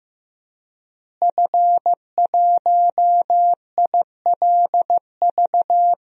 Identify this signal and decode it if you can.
Morse code, heard as a keyed tone, about 15 wpm, 700 Hz. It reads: F1ILV
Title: DXCC-100 Call Signs - 5 Characters - Repeated 3x - 15wpm